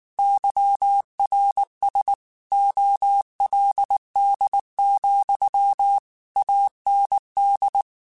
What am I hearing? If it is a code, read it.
Morse code, heard as a keyed tone, about 19 wpm, 800 Hz. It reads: YRS OLD, AND
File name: QSOs / QSOs-mp3s / WA7QOX 2@18